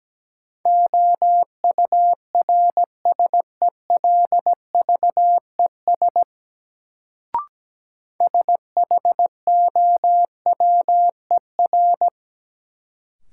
Morse code, heard as a keyed tone, about 17 wpm, 700 Hz. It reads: OURSELVES SHOWER
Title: Sentences from Top 1600 Words - New Word Drill Alt - 17wpm